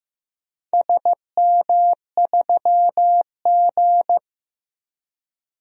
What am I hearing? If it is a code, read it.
Morse code, heard as a keyed tone, about 15 wpm, 700 Hz. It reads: SM3G
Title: DXCC-100 Call Signs - 4 Characters - 15wpm